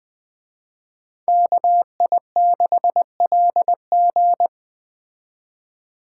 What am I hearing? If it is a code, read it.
Morse code, heard as a keyed tone, about 20 wpm, 700 Hz. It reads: KI6LG
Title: Speed Racing - USA 2x2 Call Signs 17wpm